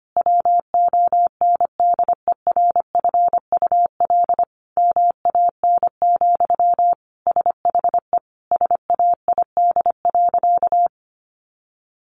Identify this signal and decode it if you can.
Morse code, heard as a keyed tone, about 25 wpm, 700 Hz. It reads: WONDERFUL MAN, SHE SAID.
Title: Sentences from Top 1100 Words 25wpm